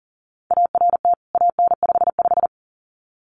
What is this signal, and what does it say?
Morse code, heard as a keyed tone, about 40 wpm, 700 Hz. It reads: ART AN55
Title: CWT Exchange 40wpm